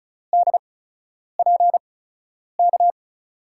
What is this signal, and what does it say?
Morse code, heard as a keyed tone, about 36 wpm, spaced as for 10 wpm, 700 Hz. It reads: DPK
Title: USA Call Signs - 6 Characters - 8x Character Spacing - 35wpm